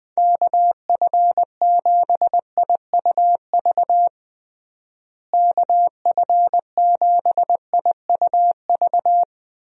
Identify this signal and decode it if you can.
Morse code, heard as a keyed tone, about 20 wpm, 700 Hz. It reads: KF7IUV KF7IUV
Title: USA Call Signs - 6 Characters - Repeated 2x - 20wpm